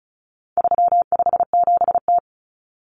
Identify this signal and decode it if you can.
Morse code, heard as a keyed tone, about 34 wpm, 700 Hz. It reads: 357T
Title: Common Cut Numbers - 4-Digit 35wpm